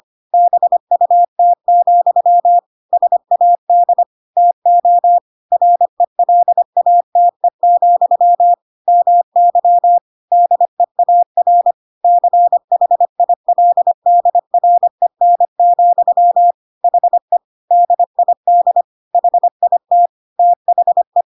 Morse code, heard as a keyed tone, about 25 wpm, 700 Hz. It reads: BUT, SAD TO RELATE, MY DEAR CHILDREN, HE DID HIT THE
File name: Pinocchio / Pinocchio ch4 25 25 ik1hge